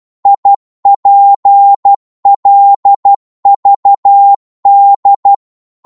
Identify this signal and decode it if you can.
Morse code, heard as a keyed tone, about 12 wpm, 800 Hz. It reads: IPLVD
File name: Morse / ltr-12wpm